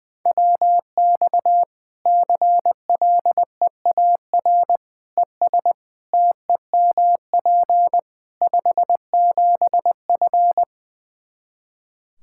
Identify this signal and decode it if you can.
Morse code, heard as a keyed tone, about 19 wpm, 700 Hz. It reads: WX CLEAR ES TEMP 57F
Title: Practice QSO - Exchange 2 - Info - 20wpm